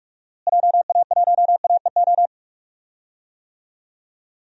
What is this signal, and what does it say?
Morse code, heard as a keyed tone, about 46 wpm, 690 Hz. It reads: JA1AEO
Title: DXCC-100 Call Signs - 6 Characters - Repeated 2x - 45wpm